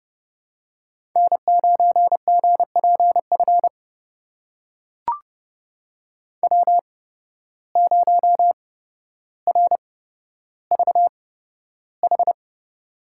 Morse code, heard as a keyed tone, about 30 wpm, spaced as for 13 wpm, 700 Hz. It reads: N9GPF W0RVH
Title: USA Call Signs - 5 Characters - 8x Character Spacing - 30wpm